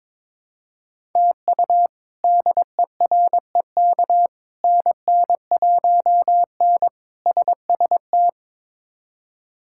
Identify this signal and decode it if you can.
Morse code, heard as a keyed tone, about 22 wpm, 700 Hz. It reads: TU DEREK NN1N SST
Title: K1USN SST QSO Practice - 22wpm